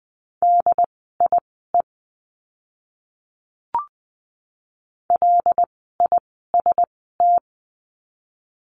Morse code, heard as a keyed tone, about 20 wpm, spaced as for 15 wpm, 700 Hz. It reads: DIE LIST
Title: 075 Lesson 13f - Letter L - Words 20wpm